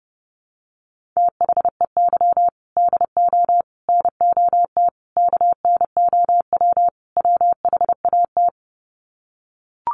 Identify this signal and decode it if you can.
Morse code, heard as a keyed tone, about 28 wpm, 700 Hz. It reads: THEY DO NOT KNOW WHAT
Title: Sets of 5 Words Encore 30wpm